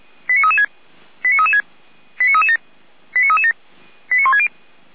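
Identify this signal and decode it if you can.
Five-tone selective call (ZVEI), heard as quick runs of five tones, tones of about 70 ms. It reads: 89336 89336 89337 89337 89160
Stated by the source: ZVEI five-tone selective call 89165 received via SDR.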